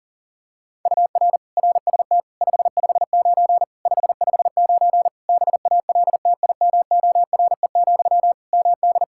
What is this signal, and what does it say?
Morse code, heard as a keyed tone, about 40 wpm, 700 Hz. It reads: UR RST 559 559 BALTIMORE, MD
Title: Practice QSO - Exchange 1 - Beginning - 40wpm